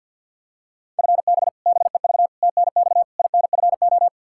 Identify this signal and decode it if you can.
Morse code, heard as a keyed tone, about 52 wpm, 700 Hz. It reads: FB BEV TNX INFO
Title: Practice QSO - Exchange 3 - More Info - 50wpm